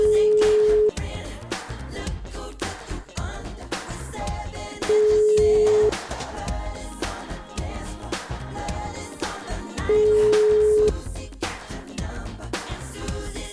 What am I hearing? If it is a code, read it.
Ringback tone (Europe, 425 Hz).